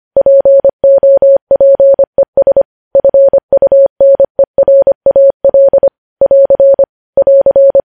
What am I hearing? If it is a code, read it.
Morse code, heard as a keyed tone, about 25 wpm, 550 Hz. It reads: POPES FUNERAL <AR> <AR>
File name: podcast / News headlines-1x-25x25